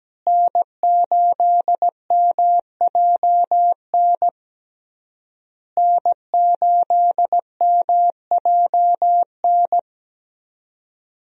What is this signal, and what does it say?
Morse code, heard as a keyed tone, about 17 wpm, 700 Hz. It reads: N8MJN N8MJN
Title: USA Call Signs - 5 Characters - Repeated 3x - 17wpm